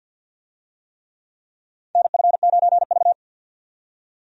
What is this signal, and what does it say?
Morse code, heard as a keyed tone, about 53 wpm, 700 Hz. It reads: NF9V
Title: USA Call Signs - 4 Characters - 50wpm